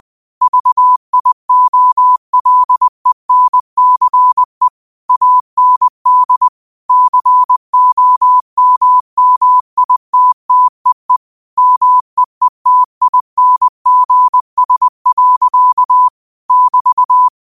Morse code, heard as a keyed tone, about 20 wpm, 1000 Hz. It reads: VIOLENCE AND COMMITTEE MEETINGS. <BT>
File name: qotdfiles / 20wpm